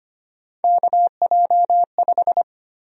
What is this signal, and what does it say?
Morse code, heard as a keyed tone, about 25 wpm, 700 Hz. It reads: KJ5
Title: USA Call Sign Prefixes - 25wpm